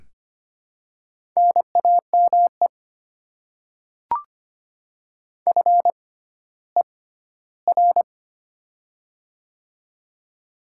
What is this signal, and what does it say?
Morse code, heard as a keyed tone, about 25 wpm, spaced as for 14 wpm, 700 Hz. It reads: NAME FER
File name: Top 20 QSO Elements - 6x Character Spacing - 25wpm